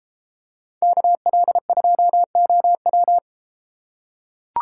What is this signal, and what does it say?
Morse code, heard as a keyed tone, about 30 wpm, 700 Hz. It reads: KL2OW